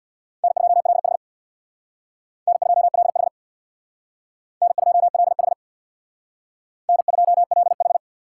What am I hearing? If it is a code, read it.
Morse code, heard as a keyed tone, about 61 wpm, 700 Hz. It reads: N2LH N2LH N2LH N2LH
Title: Speed Racing - USA 1x2 Call Signs 50wpm